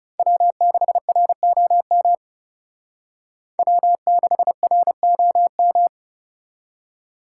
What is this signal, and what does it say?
Morse code, heard as a keyed tone, about 38 wpm, 700 Hz. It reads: W6ROM W6ROM